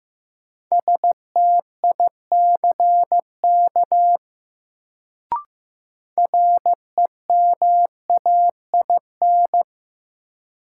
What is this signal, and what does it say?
Morse code, heard as a keyed tone, about 15 wpm, 700 Hz. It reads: STICK REMAIN
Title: Sentences from Top 1200 Words - New Word Drill Alt - 15wpm